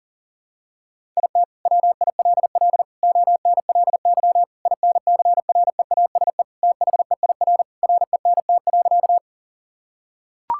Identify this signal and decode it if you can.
Morse code, heard as a keyed tone, about 40 wpm, 700 Hz. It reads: IT WILL ONLY INCREASE THEIR RENT.
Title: Sentences from Top 1000 Words - Review Part 5 - 40wpm